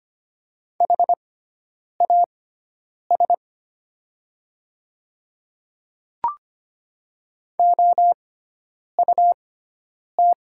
Morse code, heard as a keyed tone, about 25 wpm, spaced as for 9 wpm, 700 Hz. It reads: HAS OUT
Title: 6x Character Spacing - Top 100 Words - 25wpm